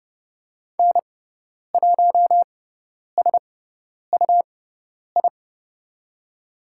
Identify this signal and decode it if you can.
Morse code, heard as a keyed tone, about 30 wpm, spaced as for 10 wpm, 690 Hz. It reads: N1SUI
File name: Lesson 18g - Letter m - Call Signs 30wpm